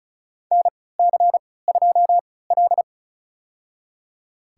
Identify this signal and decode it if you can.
Morse code, heard as a keyed tone, about 36 wpm, spaced as for 20 wpm, 700 Hz. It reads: NC2L
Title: USA Call Signs - 4 Characters - 3x Character Spacing - 35wpm